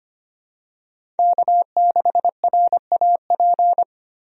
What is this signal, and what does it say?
Morse code, heard as a keyed tone, about 25 wpm, 700 Hz. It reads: K6RAP
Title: USA Call Signs - 5 Characters - 25wpm